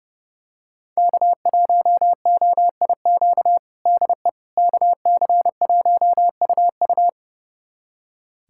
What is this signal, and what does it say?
Morse code, heard as a keyed tone, about 30 wpm, 700 Hz. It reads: K1OIQ DE KC1UU
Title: USA Callsign DE Callsign - 5 Characters - 30wpm